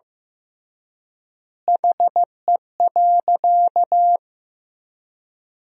Morse code, heard as a keyed tone, about 15 wpm, 700 Hz. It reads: HE.
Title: 3x Word-Spacing - Sentences from Top 100 Words - 15wpm